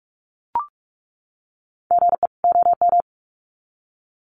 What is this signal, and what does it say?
Morse code, heard as a keyed tone, about 44 wpm, 700 Hz. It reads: GE OM
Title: Common QSO Phrases 45wpm